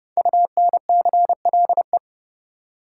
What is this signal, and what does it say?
Morse code, heard as a keyed tone, about 30 wpm, 700 Hz. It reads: UNCLE
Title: Five Letter Words - 30wpm